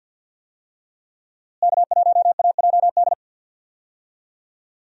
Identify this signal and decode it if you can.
Morse code, heard as a keyed tone, about 48 wpm, 700 Hz. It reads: K1AJD
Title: USA Call Signs - 5 Characters - Repeated 2x - 50wpm